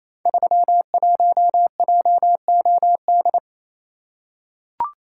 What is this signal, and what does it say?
Morse code, heard as a keyed tone, about 27 wpm, 700 Hz. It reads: 31JOD
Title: Code Groups - 5 Letter:Numbers - 28wpm